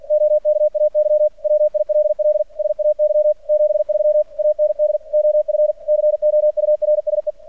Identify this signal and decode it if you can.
Morse code, heard as a keyed tone, about 48 wpm, 600 Hz. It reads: OKAY WILL SAY 73 AND OF COURSE
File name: archived / CRSnet